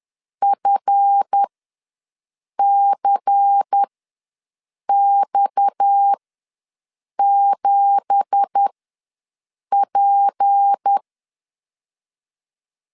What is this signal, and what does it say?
Morse code, heard as a keyed tone, about 10 wpm, spaced as for 6 wpm, 800 Hz. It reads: FCX7P
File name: LEZIONI / 9 2867 ws 10wpm 4215